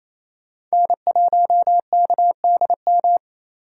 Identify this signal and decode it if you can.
Morse code, heard as a keyed tone, about 35 wpm, 700 Hz. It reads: N1KDM
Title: Speed Racing - USA 1x3 Call Signs 28wpm